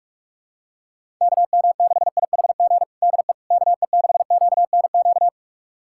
Morse code, heard as a keyed tone, about 46 wpm, 700 Hz. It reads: KM6IHG DE KE6QNQ
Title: USA Callsign DE Callsign - 6 Characters - 45wpm